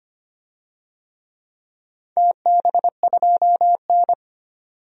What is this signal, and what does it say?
Morse code, heard as a keyed tone, about 25 wpm, 700 Hz. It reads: TB2N
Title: DXCC-100 Call Signs - 4 Characters - 25wpm